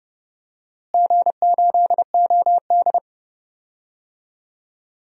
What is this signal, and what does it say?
Morse code, heard as a keyed tone, about 29 wpm, 700 Hz. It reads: G8OD
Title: DXCC-100 Call Signs - 4 Characters - Repeated 3x - 30wpm